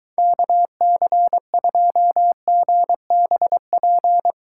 Morse code, heard as a keyed tone, about 23 wpm, 700 Hz. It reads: KC2GBP